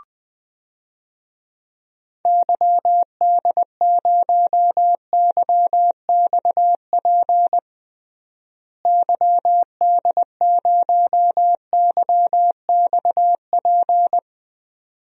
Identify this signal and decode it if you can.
Morse code, heard as a keyed tone, about 20 wpm, 700 Hz. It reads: YD0YXP YD0YXP
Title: DXCC-100 Call Signs - 6 Characters - Repeated 2x - 20wpm